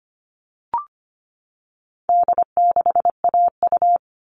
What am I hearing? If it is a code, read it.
Morse code, heard as a keyed tone, about 25 wpm, 700 Hz. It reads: D6AU
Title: All Cut Numbers - 4-Digit 25wpm